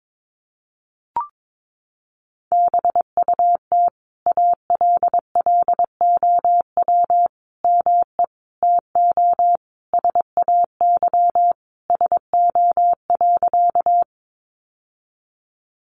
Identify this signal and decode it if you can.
Morse code, heard as a keyed tone, about 22 wpm, 700 Hz. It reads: BUT ALLOW ME TO SAY SO.
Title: Sentences from Top 700 Words 22wpm